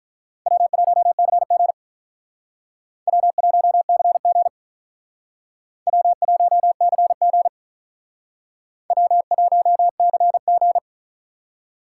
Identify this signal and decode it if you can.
Morse code, heard as a keyed tone, about 44 wpm, 700 Hz. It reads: W1CG W1CG W1CG W1CG